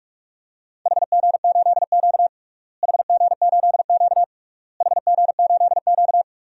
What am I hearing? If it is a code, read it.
Morse code, heard as a keyed tone, about 42 wpm, 700 Hz. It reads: HG8Q HG8Q HG8Q
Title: DXCC-100 Call Signs - 4 Characters - Repeated 3x - 45wpm